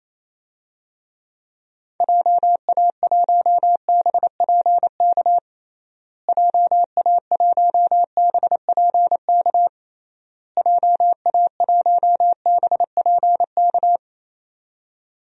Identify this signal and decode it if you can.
Morse code, heard as a keyed tone, about 28 wpm, 700 Hz. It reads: JA1BPK JA1BPK JA1BPK
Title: DXCC-100 Call Signs - 6 Characters - Repeated 3x - 28wpm